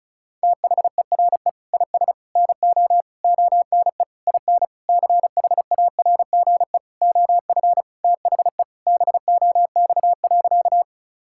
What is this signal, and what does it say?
Morse code, heard as a keyed tone, about 36 wpm, 690 Hz. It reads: THERE IS NO ONE IN CHARGE OF THE BOX.
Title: Sentences from Top 1000 Words - Review Part 6 - 35wpm